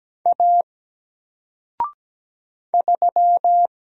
Morse code, heard as a keyed tone, about 17 wpm, 700 Hz. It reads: A 3
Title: Single Letter-Number Alt 17wpm